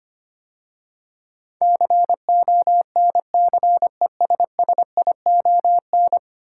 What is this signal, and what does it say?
Morse code, heard as a keyed tone, about 25 wpm, 700 Hz. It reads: CONCESSION
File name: 10 Letter Words - First 3 Letters Given - 25wpm